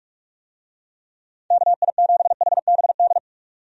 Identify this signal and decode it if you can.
Morse code, heard as a keyed tone, about 46 wpm, 700 Hz. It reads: KI7HBD
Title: USA Call Signs - 6 Characters - 45wpm